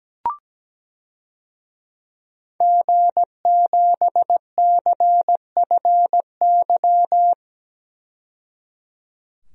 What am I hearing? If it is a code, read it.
Morse code, heard as a keyed tone, about 17 wpm, 700 Hz. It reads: G7CFY
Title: DXCC-100 Call Signs - 5 Characters - 17wpm